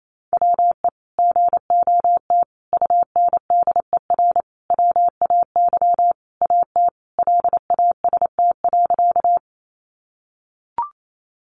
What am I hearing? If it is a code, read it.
Morse code, heard as a keyed tone, about 28 wpm, 700 Hz. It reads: WE GOT UNDER WAY AT LAST.
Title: Sentences from Top 300 Words 28wpm